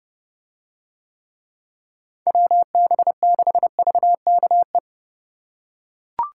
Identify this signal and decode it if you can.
Morse code, heard as a keyed tone, about 28 wpm, 700 Hz. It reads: WB6VKE